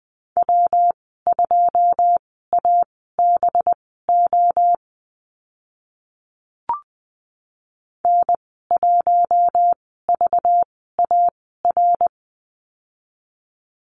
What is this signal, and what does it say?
Morse code, heard as a keyed tone, about 19 wpm, spaced as for 14 wpm, 700 Hz. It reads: W2ABO N1VAR
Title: 193 Lesson 31g - Letter v - Call Signs 20wpm